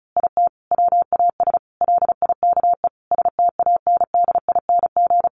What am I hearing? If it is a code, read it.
Morse code, heard as a keyed tone, about 34 wpm, 700 Hz. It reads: IT WAS LIKE STANDING
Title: A Sound Of Thunder 35wpm